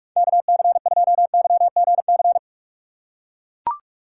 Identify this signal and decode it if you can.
Morse code, heard as a keyed tone, about 43 wpm, 700 Hz. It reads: KC2YGC
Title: USA Call Signs - 4x Character Spacing - 45wpm